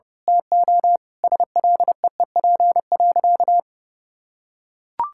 Morse code, heard as a keyed tone, about 30 wpm, 700 Hz. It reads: TO SLEEP.
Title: Sentences from Top 1600 Words 30wpm